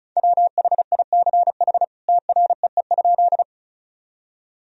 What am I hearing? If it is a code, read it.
Morse code, heard as a keyed tone, about 34 wpm, 700 Hz. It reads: WHICH TREE?
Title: Sentences from Top 1000 Words - Review Part 1 - 35wpm